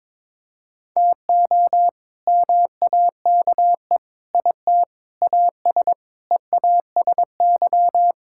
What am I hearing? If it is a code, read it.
Morse code, heard as a keyed tone, about 22 wpm, 700 Hz. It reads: TO MAKE IT AS EASY
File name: practice / 22wpm / Sets of 8 Words - 22wpm